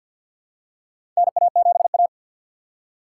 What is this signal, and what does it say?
Morse code, heard as a keyed tone, about 51 wpm, 700 Hz. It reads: NA7A
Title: USA Call Signs - 4 Characters - 6x Character Spacing - 50wpm